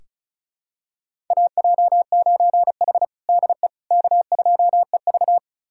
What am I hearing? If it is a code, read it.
Morse code, heard as a keyed tone, about 34 wpm, 700 Hz. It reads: AJ9H DE K2EV
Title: USA Callsign DE Callsign - 4 Characters - 35wpm